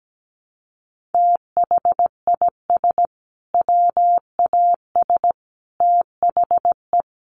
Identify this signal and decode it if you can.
Morse code, heard as a keyed tone, about 17 wpm, 700 Hz. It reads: THIS WAS THE
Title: Sentences from Top 1500 Words 17wpm